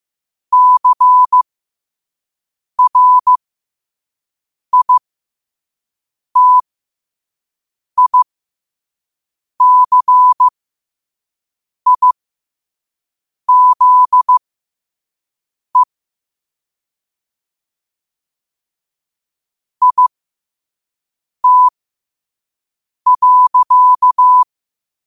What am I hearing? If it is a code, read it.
Morse code, heard as a keyed tone, about 15 wpm, spaced as for 5 wpm, 1000 Hz. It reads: CRITICIZE IT.
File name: qotdfiles / 5wpm